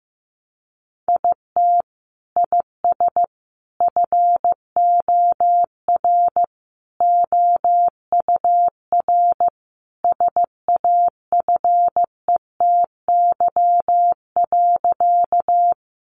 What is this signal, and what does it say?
Morse code, heard as a keyed tone, about 15 wpm, 700 Hz. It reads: IT IS FOR OUR SAFETY.
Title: Sentences from Top 700 Words 15wpm